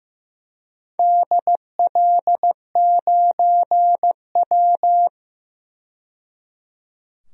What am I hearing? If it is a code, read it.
Morse code, heard as a keyed tone, about 15 wpm, 700 Hz. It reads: DL9W
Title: DXCC-100 Call Signs - 4 Characters - 15wpm